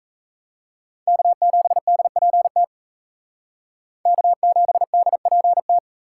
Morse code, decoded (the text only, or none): K7DPT K7DPT